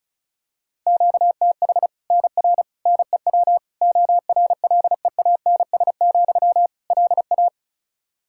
Morse code, heard as a keyed tone, about 34 wpm, 700 Hz. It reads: QTH NR NEW ORLEANS, LA
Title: Practice QSO - Exchange 1 - Beginning - 35wpm